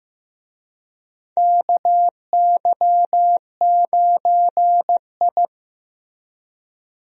Morse code, heard as a keyed tone, about 15 wpm, 700 Hz. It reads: KY9I
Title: USA Call Signs - 4 Characters - Repeated 3x - 15wpm